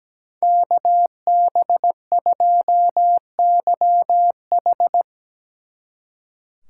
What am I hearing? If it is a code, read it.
Morse code, heard as a keyed tone, about 20 wpm, 700 Hz. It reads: KB2YH
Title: Speed Racing - USA 2x2 Call Signs 17wpm